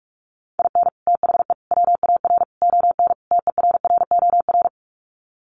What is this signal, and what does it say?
Morse code, heard as a keyed tone, about 44 wpm, 700 Hz. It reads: IN THE WAR ON TERROR
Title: Sets of 5 Words Encore2 45wpm